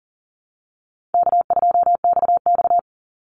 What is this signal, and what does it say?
Morse code, heard as a keyed tone, about 40 wpm, 700 Hz. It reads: K2XX
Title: Call Signs Repeated 3x - 40wpm